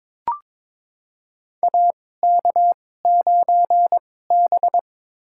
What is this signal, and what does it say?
Morse code, heard as a keyed tone, about 22 wpm, spaced as for 16 wpm, 700 Hz. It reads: AK9B
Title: USA Call Signs - 4 Characters - 2x Character Spacing - 22wpm